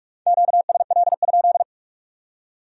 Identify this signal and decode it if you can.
Morse code, heard as a keyed tone, about 55 wpm, 700 Hz. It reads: QSL?
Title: Speed-Racing Q-Codes - 45wpm